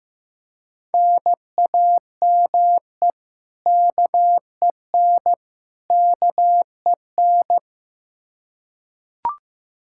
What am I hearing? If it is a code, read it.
Morse code, heard as a keyed tone, about 15 wpm, 700 Hz. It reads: NAME KEN KEN
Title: Practice QSO - Exchange 1 - Beginning - 15wpm